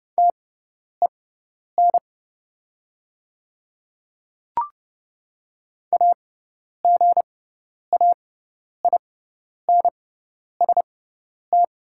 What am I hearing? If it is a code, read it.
Morse code, heard as a keyed tone, about 30 wpm, spaced as for 10 wpm, 690 Hz. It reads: TEN AGAINST
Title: 6x Character Spacing - Top 500 Words - 30wpm